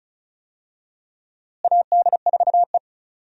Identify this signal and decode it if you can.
Morse code, heard as a keyed tone, about 36 wpm, 700 Hz. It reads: AD4E
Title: USA Call Signs - 4 Characters - Repeated 2x - 35wpm